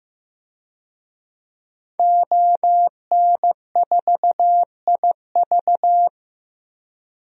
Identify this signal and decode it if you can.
Morse code, heard as a keyed tone, about 15 wpm, 700 Hz. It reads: ON4IV
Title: DXCC-100 Call Signs - 5 Characters - Repeated 3x - 15wpm